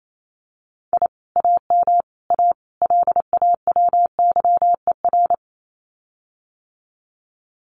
Morse code, heard as a keyed tone, about 28 wpm, 700 Hz. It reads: I AM A LAWYER
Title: Common QSO Phrases Encore 28wpm